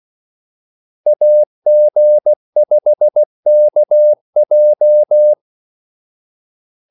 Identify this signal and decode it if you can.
Morse code, heard as a keyed tone, about 16 wpm, 600 Hz. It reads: AG5KJ
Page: Kaldesignaler 16 wpm